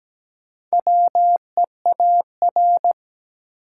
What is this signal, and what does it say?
Morse code, heard as a keyed tone, about 17 wpm, 700 Hz. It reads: WEAR
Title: Sentences from Top 1000 Words - Review Part 6 - New Word Drill - 17wpm